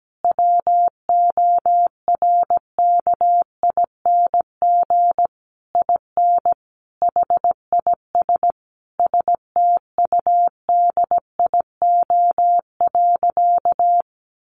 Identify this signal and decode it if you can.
Morse code, heard as a keyed tone, about 17 wpm, 700 Hz. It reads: WORKING IN HIS STUDIO.
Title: Sentences from Top 1200 Words 17wpm